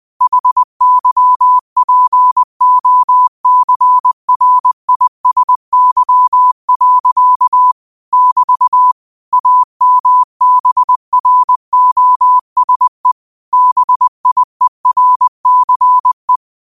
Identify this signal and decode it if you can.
Morse code, heard as a keyed tone, about 20 wpm, 1000 Hz. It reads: HYPOCRISY. <BT> AMBROSE BIERCE